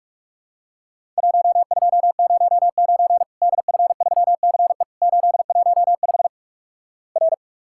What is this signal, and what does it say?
Morse code, heard as a keyed tone, about 46 wpm, 700 Hz. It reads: 1209 DF3CE 815 R
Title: Worked All Europe DX Contest - QTC Exchange - 45wpm